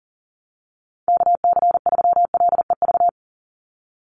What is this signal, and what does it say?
Morse code, heard as a keyed tone, about 40 wpm, 700 Hz. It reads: KC3LEV
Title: USA Call Signs 40wpm